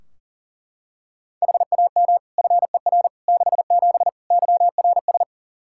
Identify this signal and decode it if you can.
Morse code, heard as a keyed tone, about 40 wpm, 700 Hz. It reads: HAM FER 67 YRS
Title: Practice QSO - Exchange 3 - More Info - 40wpm